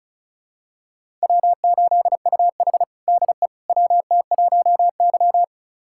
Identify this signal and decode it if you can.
Morse code, heard as a keyed tone, about 34 wpm, 700 Hz. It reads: W8UH DE WT1Y
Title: USA Callsign DE Callsign - 4 Characters - 35wpm